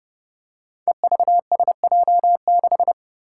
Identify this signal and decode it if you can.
Morse code, heard as a keyed tone, about 30 wpm, 700 Hz. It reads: EVSJ6
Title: Five Challenging Letter-Number Combinations 30wpm